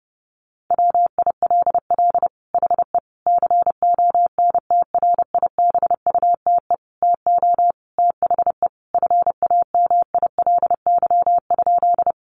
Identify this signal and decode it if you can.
Morse code, heard as a keyed tone, about 30 wpm, 700 Hz. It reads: WILL HE CONTRIBUTE TO THE FAMILY?
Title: Sentences from Top 1400 Words 30wpm